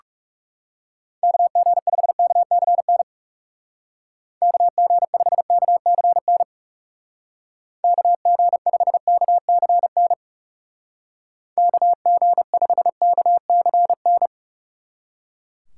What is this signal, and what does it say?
Morse code, heard as a keyed tone, about 38 wpm, 700 Hz. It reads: KG5KCN KG5KCN KG5KCN KG5KCN